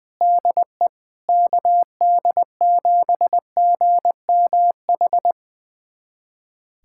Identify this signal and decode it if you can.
Morse code, heard as a keyed tone, about 21 wpm, 700 Hz. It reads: DE KD7GMH
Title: USA Callsign DE Callsign - 6 Characters - 20wpm